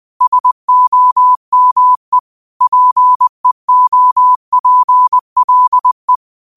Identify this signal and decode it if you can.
Morse code, heard as a keyed tone, about 20 wpm, 1000 Hz. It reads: SOME PEOPLE
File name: qotdfiles / 20wpm